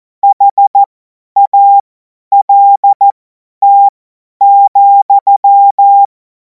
Morse code, heard as a keyed tone, about 14 wpm, spaced as for 10 wpm, 800 Hz. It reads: HALT,
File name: Morsekurs / Witze / Witz 19. Mai 2026 50 F70